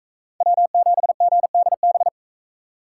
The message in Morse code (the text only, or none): W7GDB